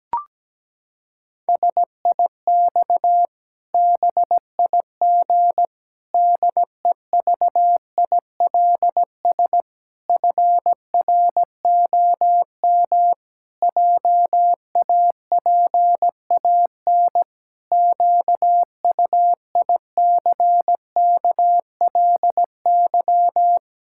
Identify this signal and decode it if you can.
Morse code, heard as a keyed tone, about 17 wpm, 700 Hz. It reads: SIX BIG DEVILS FROM JAPAN QUICKLY
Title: Pangram Challenge v2 - 17wpm